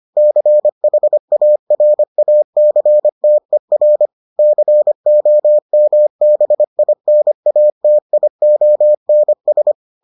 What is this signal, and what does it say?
Morse code, heard as a keyed tone, about 25 wpm, 600 Hz. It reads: CHARACTER COMBINATIONS